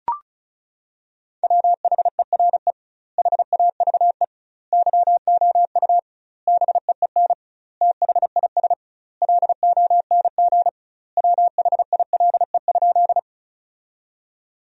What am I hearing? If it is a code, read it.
Morse code, heard as a keyed tone, about 34 wpm, 700 Hz. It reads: WHERE HAVE YOU BEEN THIS LONG WHILE?
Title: 2x Word-Spacing - Sentences from Top 300 Words - 35wpm